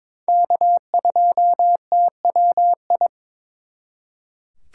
Morse code, heard as a keyed tone, about 26 wpm, 700 Hz. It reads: K2TWI
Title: Speed Racing - USA 1x3 Call Signs 22wpm